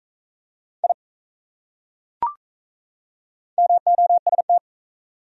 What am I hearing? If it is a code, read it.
Morse code, heard as a keyed tone, about 35 wpm, 700 Hz. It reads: I MOST
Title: Speed-Racing Top 100 Words 28wpm